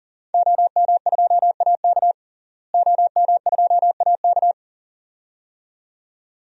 Morse code, heard as a keyed tone, about 40 wpm, 700 Hz. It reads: OM2AK OM2AK
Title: DXCC-100 Call Signs - 5 Characters - Repeated 2x - 40wpm